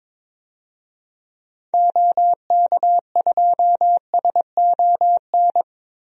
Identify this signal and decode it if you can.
Morse code, heard as a keyed tone, about 22 wpm, 700 Hz. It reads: OK2SON
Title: DXCC-100 Call Signs - 6 Characters - 22wpm